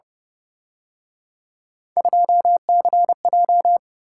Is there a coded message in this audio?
Yes: Morse code from 2.0 to 3.8 s, heard as a keyed tone, about 29 wpm, 700 Hz. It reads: 2CJ